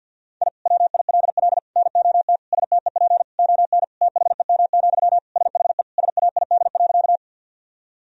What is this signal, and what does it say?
Morse code, heard as a keyed tone, about 52 wpm, 700 Hz. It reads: I WILL NOT STEP ON THEM, SHE SAID.
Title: Sentences from Top 1000 Words - Review Part 2 - 50wpm